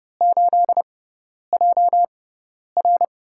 Morse code, heard as a keyed tone, about 30 wpm, spaced as for 10 wpm, 700 Hz. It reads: 8JR
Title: USA Call Signs - 5 Characters - 6x Character Spacing - 30wpm